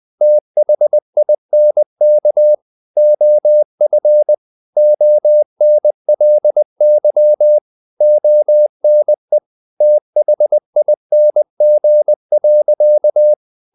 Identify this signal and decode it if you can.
Morse code, heard as a keyed tone, about 20 wpm, 600 Hz. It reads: THINK OF ONLY ONE THING.
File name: the old man and the sea / the old man and the sea part 1 20wpm